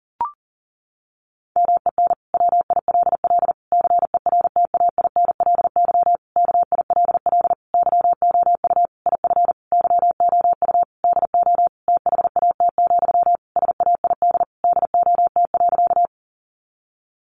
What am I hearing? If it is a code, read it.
Morse code, heard as a keyed tone, about 40 wpm, 700 Hz. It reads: MEN WILL CERTAINLY KILL YOU IF YOU DO THAT, SAID DOT.
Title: Sentences from Top 1500 Words 40wpm